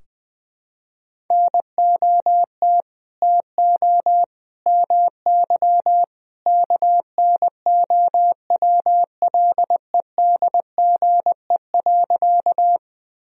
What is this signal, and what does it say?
Morse code, heard as a keyed tone, about 21 wpm, 700 Hz. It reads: NOT TO MY KNOWLEDGE.
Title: Sentences from Top 1000 Words - Review Part 3 - 20wpm